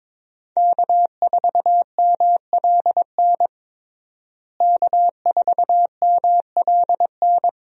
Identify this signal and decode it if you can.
Morse code, heard as a keyed tone, about 21 wpm, 700 Hz. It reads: K4MLN K4MLN
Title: USA Call Signs - 5 Characters - Repeated 3x - 22wpm